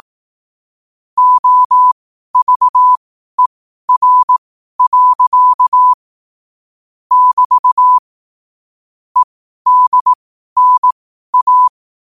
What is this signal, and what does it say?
Morse code, heard as a keyed tone, about 18 wpm, spaced as for 12 wpm, 1000 Hz. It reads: OVER. <BT> EDNA